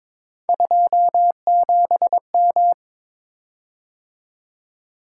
Morse code, heard as a keyed tone, about 22 wpm, 700 Hz. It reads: 27M
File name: practice / 22wpm / Three Letter-Number Combinations - 22wpm